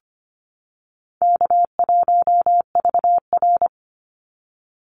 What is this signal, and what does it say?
Morse code, heard as a keyed tone, about 25 wpm, 700 Hz. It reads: K1VR
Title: Easy Call Signs - 25wpm